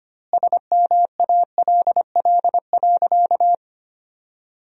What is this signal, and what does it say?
Morse code, heard as a keyed tone, about 25 wpm, 700 Hz. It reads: SMALL.
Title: 2x Word-Spacing - Sentences from Top 400 Words - 25wpm